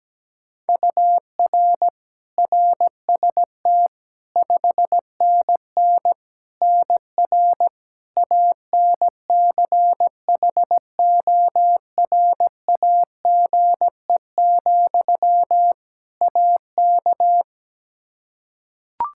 Morse code, heard as a keyed tone, about 17 wpm, 700 Hz. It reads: UR RST 5NN NR ANCHORAGE, AK
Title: Practice QSO - Exchange 1 - Beginning - 17wpm